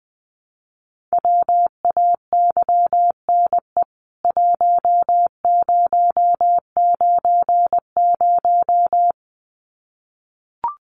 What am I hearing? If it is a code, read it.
Morse code, heard as a keyed tone, about 20 wpm, 700 Hz. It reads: WAYNE 1090
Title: CWT Exchange 20wpm